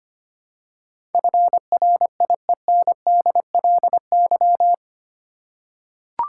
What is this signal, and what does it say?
Morse code, heard as a keyed tone, about 25 wpm, 700 Hz. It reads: FRIENDLY
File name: Eight Letter Words - 25wpm